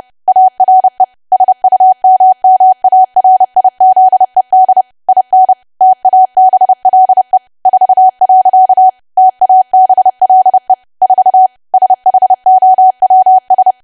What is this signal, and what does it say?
Morse code, heard as a keyed tone, about 30 wpm, 750 Hz. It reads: ARE SUMMARIZED IN TABLE 4. TABLE 4 SHOWS
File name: cwa / W1AW-Level3 / 30WPM